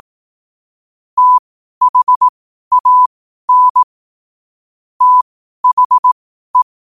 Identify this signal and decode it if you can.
Morse code, heard as a keyed tone, about 18 wpm, spaced as for 12 wpm, 1000 Hz. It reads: THAN THE
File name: qotdfiles / 12wpm